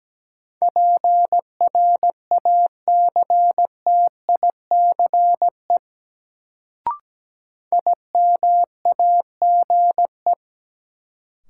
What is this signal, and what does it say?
Morse code, heard as a keyed tone, about 17 wpm, 700 Hz. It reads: PRACTICE IMAGE
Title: Sentences from Top 700 Words - New Word Drill - 17wpm